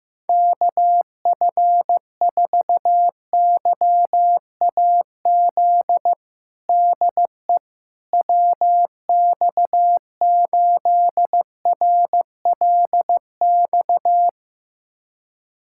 Morse code, heard as a keyed tone, about 15 wpm, 700 Hz. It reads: KF4YAZ DE WX8RLX
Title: USA Callsign DE Callsign 15wpm